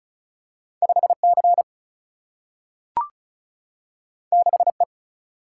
Morse code, heard as a keyed tone, about 36 wpm, 700 Hz. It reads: 5 C 6 E
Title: Single-Letter Single-Number 35wpm